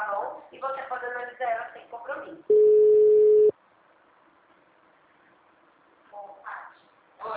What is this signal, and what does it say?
Ringback tone (Europe, 425 Hz), with speech.